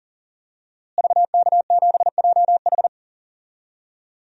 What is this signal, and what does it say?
Morse code, heard as a keyed tone, about 37 wpm, 700 Hz. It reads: VK7JH
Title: DXCC-100 Call Signs - 5 Characters - Repeated 2x - 40wpm